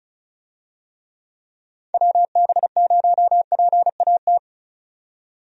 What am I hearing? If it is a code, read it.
Morse code, heard as a keyed tone, about 36 wpm, 700 Hz. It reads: WB0PAT
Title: USA Call Signs - 6 Characters - 35wpm